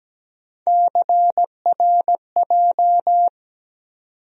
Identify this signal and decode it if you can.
Morse code, heard as a keyed tone, about 17 wpm, 700 Hz. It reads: CRJ
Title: USA Call Sign Postfixes - 3 Letters 17wpm